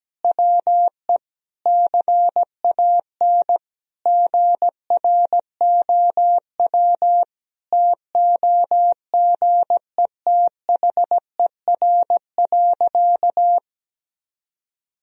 Morse code, heard as a keyed tone, about 17 wpm, 700 Hz. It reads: WE CAN GROW TOGETHER.
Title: Sentences from Top 1000 Words - Review Part 1 - 17wpm